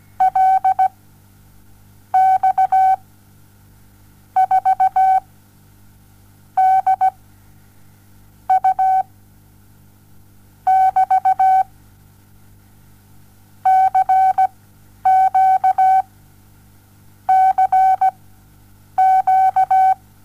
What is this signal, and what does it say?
Morse code, heard as a keyed tone, about 17 wpm, spaced as for 7 wpm, 760 Hz. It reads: LX 4 D U <BT> CQCQ
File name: Onlinelehrgang / morsen / DJ4UF Morsen07